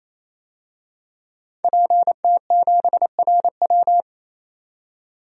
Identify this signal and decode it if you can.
Morse code, heard as a keyed tone, about 28 wpm, 700 Hz. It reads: PT7RW
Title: DXCC-100 Call Signs - 5 Characters - 28wpm